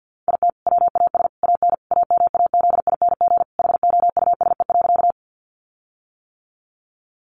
Morse code, heard as a keyed tone, about 50 wpm, 700 Hz. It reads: IT WAS AN AMAZING HOUSE.
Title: Sentences from Top 1500 Words 50wpm